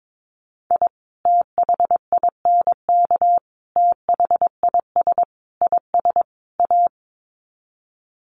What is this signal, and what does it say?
Morse code, heard as a keyed tone, about 22 wpm, 700 Hz. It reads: I THINK THIS IS A
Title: Sets of 5 Words 22wpm